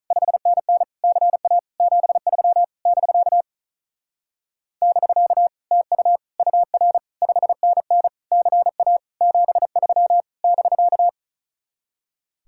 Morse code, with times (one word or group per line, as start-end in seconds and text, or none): 0.1-0.8 5NN
1.0-1.6 CA
1.8-2.6 73
2.9-3.4 <BK>
4.8-5.5 <BK>
5.7-6.2 TU
6.4-7.0 UR
7.2-8.1 5NN
8.3-9.0 CA
9.2-10.2 73
10.4-11.1 <BK>